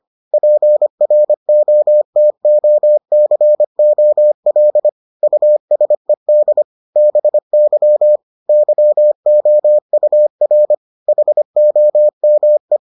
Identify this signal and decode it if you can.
Morse code, heard as a keyed tone, about 25 wpm, 600 Hz. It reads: PROTOCOL USED BY YOUR HOME